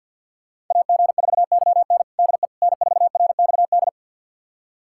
Morse code, heard as a keyed tone, about 52 wpm, 690 Hz. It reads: AG4YN DE N4RXD
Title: USA Callsign DE Callsign - 5 Characters - 50wpm